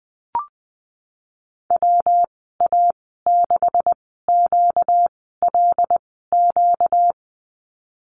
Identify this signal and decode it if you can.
Morse code, heard as a keyed tone, about 19 wpm, 700 Hz. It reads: W A 6 Q L Q
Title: 238 Lesson 38g - Letter q - Call Signs 20wpm